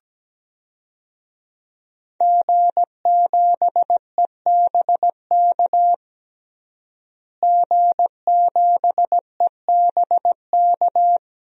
Morse code, heard as a keyed tone, about 17 wpm, 700 Hz. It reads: G7EBK G7EBK
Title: DXCC-100 Call Signs - 5 Characters - Repeated 2x - 17wpm